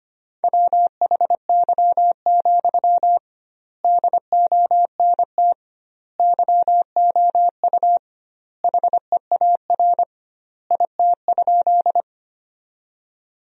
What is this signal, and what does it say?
Morse code, heard as a keyed tone, about 25 wpm, 700 Hz. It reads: WHY, DONT YOU HEAR IT?
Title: 2x Word-Spacing - Sentences from Top 300 Words - 25wpm